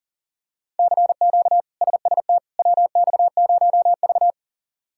Morse code, heard as a keyed tone, about 40 wpm, 700 Hz. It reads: CQ SST WX0V
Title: K1USN SST QSO Practice - 40wpm